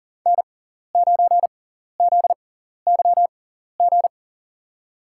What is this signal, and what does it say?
Morse code, heard as a keyed tone, about 39 wpm, spaced as for 14 wpm, 700 Hz. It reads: N9ZYG
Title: USA Call Signs - 5 Characters - 6x Character Spacing - 40wpm